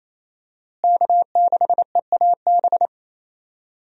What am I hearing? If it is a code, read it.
Morse code, heard as a keyed tone, about 28 wpm, 700 Hz. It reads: K6EAB
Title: USA Call Signs - 5 Characters - Repeated 2x - 28wpm